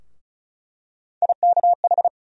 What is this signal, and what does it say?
Morse code, heard as a keyed tone, about 36 wpm, 700 Hz. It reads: IKH
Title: USA Call Sign Postfixes - 3 Letters 35wpm